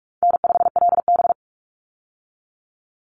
Morse code, heard as a keyed tone, about 44 wpm, 700 Hz. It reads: N5LB
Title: Easy Call Signs - 45wpm